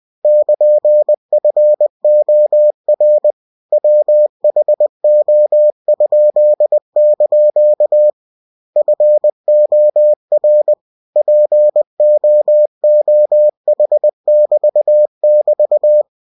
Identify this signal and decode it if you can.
Morse code, heard as a keyed tone, about 20 wpm, 600 Hz. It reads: <KN>FOR WHO?) FOR POOH<BT><BT>
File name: cw-books / winnie-the-pooh / winnie-20wpm-page-0103